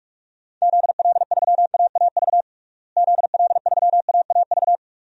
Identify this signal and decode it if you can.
Morse code, heard as a keyed tone, about 45 wpm, 700 Hz. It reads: ZL3AAV ZL3AAV
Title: DXCC-100 Call Signs - 6 Characters - Repeated 2x - 45wpm